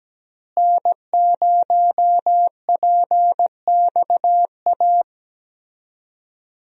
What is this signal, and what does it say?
Morse code, heard as a keyed tone, about 17 wpm, 700 Hz. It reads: N0PXA
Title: USA Call Signs - 5 Characters - Repeated 3x - 17wpm